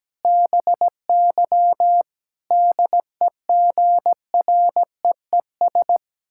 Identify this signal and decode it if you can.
Morse code, heard as a keyed tone, about 17 wpm, 700 Hz. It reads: BY DEGREES
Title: Pangram Challenge v2 - 17wpm